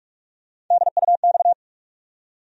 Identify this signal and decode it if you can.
Morse code, heard as a keyed tone, about 44 wpm, 700 Hz. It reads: DUX
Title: USA Call Sign Postfixes - 3 Letters 45wpm